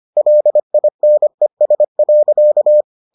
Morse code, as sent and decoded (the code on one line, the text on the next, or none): .-.. .. -. ... .-.-.-
LINES.